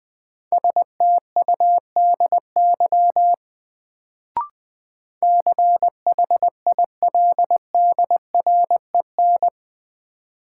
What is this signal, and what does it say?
Morse code, heard as a keyed tone, about 20 wpm, 700 Hz. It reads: STUDY CHILDREN
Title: Sentences from Top 300 Words - New Word Drill - 20wpm